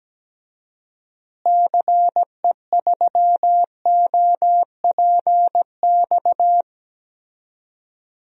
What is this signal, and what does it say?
Morse code, heard as a keyed tone, about 17 wpm, 700 Hz. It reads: CE3OPX
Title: DXCC-100 Call Signs - 6 Characters - Repeated 2x - 17wpm